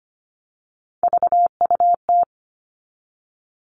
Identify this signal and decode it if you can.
Morse code, heard as a keyed tone, about 25 wpm, 700 Hz. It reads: VUT
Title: All Cut Numbers - 3-Digit 25wpm